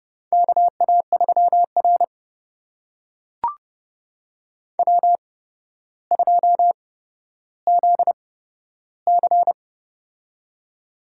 Morse code, heard as a keyed tone, about 30 wpm, spaced as for 13 wpm, 700 Hz. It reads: KA3R W2ZC
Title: USA Call Signs - 4 Characters - 8x Character Spacing - 30wpm